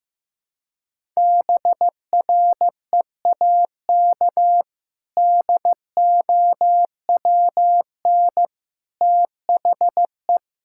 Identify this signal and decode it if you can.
Morse code, heard as a keyed tone, about 16 wpm, 700 Hz. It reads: BREAK DOWN THE
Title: Sentences from Top 1000 Words - Review Part 6 - 15wpm